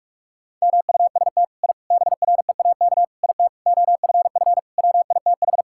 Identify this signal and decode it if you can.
Morse code, heard as a keyed tone, about 46 wpm, 700 Hz. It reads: MUST I BREAK IT OFF WITH
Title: Sentences from Top 1000 Words - Review Part 6 - 45wpm